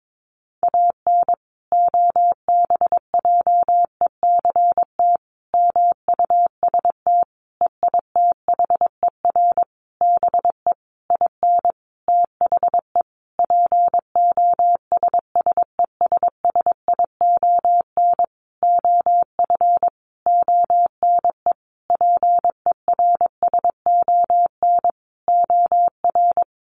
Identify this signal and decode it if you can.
Morse code, heard as a keyed tone, about 22 wpm, 700 Hz. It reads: AN OBJECT MUST EITHER BE IN THE POSSESSION OF ONE PERSON OR
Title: Sentences from Top 1400 Words 22wpm